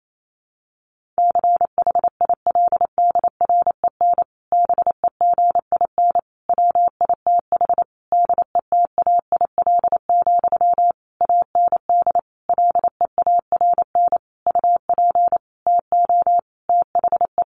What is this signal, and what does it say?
Morse code, heard as a keyed tone, about 28 wpm, 700 Hz. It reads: CHILDREN BEGIN WITH DETAIL, AND LEARN UP TO THE
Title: Sentences from Top 1500 Words 28wpm